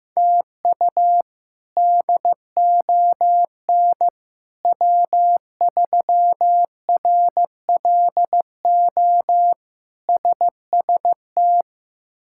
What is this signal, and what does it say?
Morse code, heard as a keyed tone, about 15 wpm, 700 Hz. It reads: TU DON W3RLO SST
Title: K1USN SST QSO Practice - 15wpm